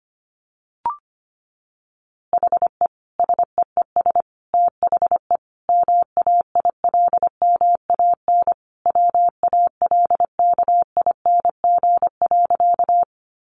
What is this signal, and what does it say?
Morse code, heard as a keyed tone, about 25 wpm, 700 Hz. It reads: HE SEES THE MAILMAN WALKING.
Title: Easy Phrases 25wpm